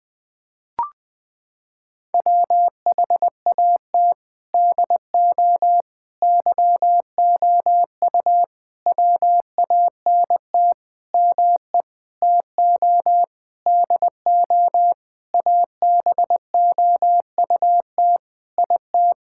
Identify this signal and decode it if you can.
Morse code, heard as a keyed tone, about 20 wpm, 700 Hz. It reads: WHAT DO YOU WANT ME TO DO ABOUT IT
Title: Sentences with 7-Word Phrases - 20wpm